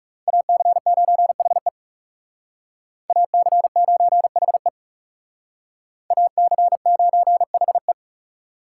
Morse code, heard as a keyed tone, about 36 wpm, 700 Hz. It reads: AC9HE AC9HE AC9HE
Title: Speed Racing - USA 2x2 Call Signs 30wpm